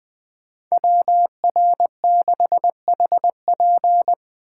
Morse code, heard as a keyed tone, about 20 wpm, 700 Hz. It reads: WR6HP
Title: USA Call Signs - 5 Characters - 6x Character Spacing - 20wpm